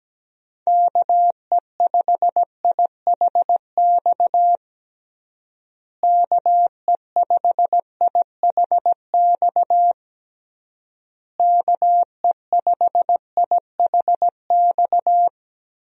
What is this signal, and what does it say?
Morse code, heard as a keyed tone, about 17 wpm, 700 Hz. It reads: KE5IHX KE5IHX KE5IHX
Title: USA Call Signs - 6 Characters - Repeated 3x - 17wpm